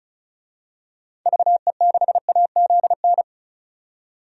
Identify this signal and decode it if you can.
Morse code, heard as a keyed tone, about 34 wpm, 700 Hz. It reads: VE6AZN
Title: DXCC-100 Call Signs - 6 Characters - 35wpm